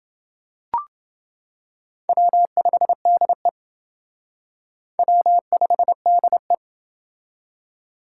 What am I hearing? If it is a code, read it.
Morse code, heard as a keyed tone, about 28 wpm, 700 Hz. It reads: W5DE W5DE